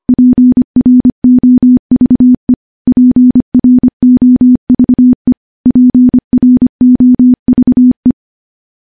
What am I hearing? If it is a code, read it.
Morse code, heard as a keyed tone, about 25 wpm, 250 Hz. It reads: PROVE PROVE PROVE
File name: morse / morse-training-25WPM